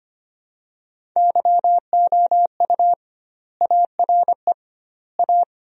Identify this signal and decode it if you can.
Morse code, heard as a keyed tone, about 25 wpm, 700 Hz. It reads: YOU ARE A
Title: 2x Word-Spacing - Sentences from Top 200 Words - 25wpm